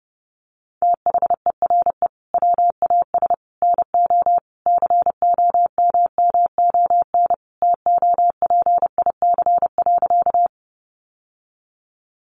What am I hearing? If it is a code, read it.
Morse code, heard as a keyed tone, about 30 wpm, 700 Hz. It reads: THERE WAS NO COMMON TOPIC.
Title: Sentences from Top 1200 Words 30wpm